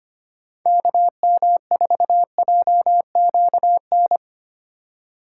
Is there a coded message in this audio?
Yes: Morse code from 0.7 to 4.2 s, heard as a keyed tone, about 31 wpm, 700 Hz. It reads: KM4JQN